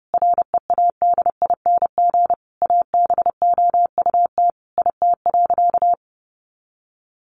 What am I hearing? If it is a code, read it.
Morse code, heard as a keyed tone, about 28 wpm, 700 Hz. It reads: READING ABOUT IT.
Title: Sentences from Top 600 Words 30wpm